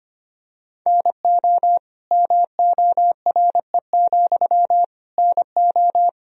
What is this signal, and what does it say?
Morse code, heard as a keyed tone, about 25 wpm, 700 Hz. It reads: NO MORE, NO
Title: Sentences from Top 1000 Words - Review Part 2 - 25wpm